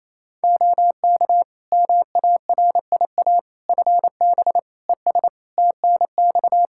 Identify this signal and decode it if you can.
Morse code, heard as a keyed tone, about 28 wpm, 700 Hz. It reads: OK MARIA FB ES TNX
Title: Practice QSO - Exchange 2 - Info - 28wpm